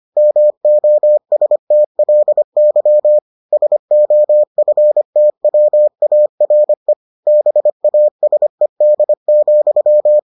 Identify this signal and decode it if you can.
Morse code, heard as a keyed tone, about 25 wpm, 600 Hz. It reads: MOSTLY SOFTWARE BASED,